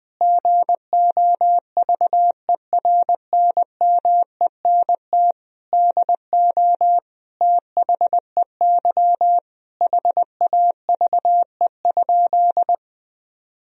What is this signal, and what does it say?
Morse code, heard as a keyed tone, about 20 wpm, 700 Hz. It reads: GOVERNMENT DO THEY HAVE?
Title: Sentences from Top 1000 Words - Review Part 3 - 20wpm